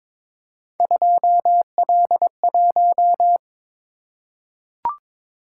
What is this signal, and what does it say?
Morse code, heard as a keyed tone, about 22 wpm, 700 Hz. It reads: 2L1
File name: practice / 22wpm / Three Letter-Number Combinations - 22wpm